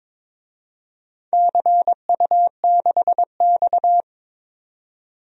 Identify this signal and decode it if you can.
Morse code, heard as a keyed tone, about 22 wpm, 700 Hz. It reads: CU6X
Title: DXCC-100 Call Signs - 4 Characters - 22wpm